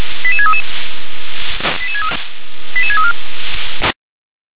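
Five-tone selective call (ZVEI), heard as quick runs of five tones, tones of about 80 ms. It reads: 88633 88634
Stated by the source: ZVEI five-tone call received by SDR.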